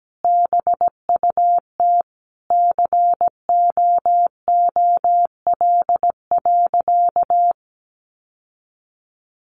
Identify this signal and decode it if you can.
Morse code, heard as a keyed tone, about 17 wpm, 700 Hz. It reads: BUT COOL.
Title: Sentences from Top 1200 Words 17wpm